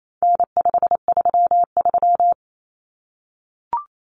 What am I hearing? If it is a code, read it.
Morse code, heard as a keyed tone, about 28 wpm, 700 Hz. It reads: N533
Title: Common Cut Numbers - 4-Digit 28wpm